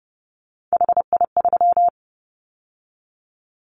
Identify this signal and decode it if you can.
Morse code, heard as a keyed tone, about 30 wpm, 700 Hz. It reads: HI3
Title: Callsign Prefixes for Top 100 DXCC - 30wpm